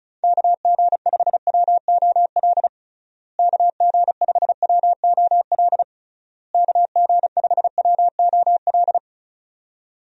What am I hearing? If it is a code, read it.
Morse code, heard as a keyed tone, about 34 wpm, 700 Hz. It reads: KG5WOL KG5WOL KG5WOL
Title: USA Call Signs - 6 Characters - Repeated 3x - 35wpm